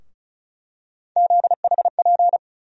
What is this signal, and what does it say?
Morse code, heard as a keyed tone, about 36 wpm, 700 Hz. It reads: ZHP
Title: USA Call Sign Postfixes - 3 Letters 35wpm